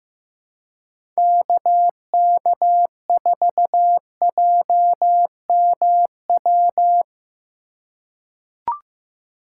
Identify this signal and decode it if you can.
Morse code, heard as a keyed tone, about 15 wpm, 700 Hz. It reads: KK4JMW
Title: USA Call Signs - 6 Characters - 2x Character Spacing - 15wpm